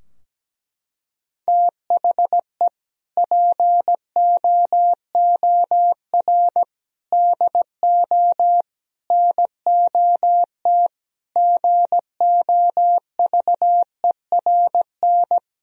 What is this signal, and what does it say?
Morse code, heard as a keyed tone, about 17 wpm, 700 Hz. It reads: THE POOR DO NOT GOVERN
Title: Sentences from Top 1000 Words - Review Part 2 - 17wpm